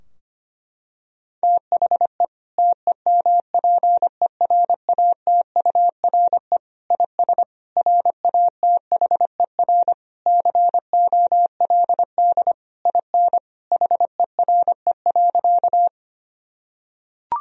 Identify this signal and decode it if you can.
Morse code, heard as a keyed tone, about 25 wpm, 700 Hz. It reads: THE TEMPERATURE IS RATHER COLD IN HERE.
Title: Sentences from Top 1000 Words - Review Part 4 - 25wpm